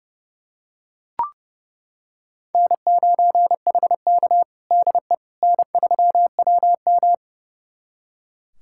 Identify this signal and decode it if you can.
Morse code, heard as a keyed tone, about 32 wpm, 700 Hz. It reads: N9HK DE N3WM
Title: USA Callsign DE Callsign - 4 Characters - 30wpm